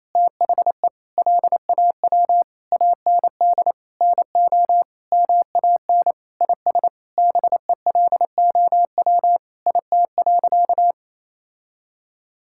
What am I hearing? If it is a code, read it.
Morse code, heard as a keyed tone, about 28 wpm, 700 Hz. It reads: THE LAW AND NO MAN IS BELOW IT.
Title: Sentences from Top 1000 Words - Review Part 5 - 28wpm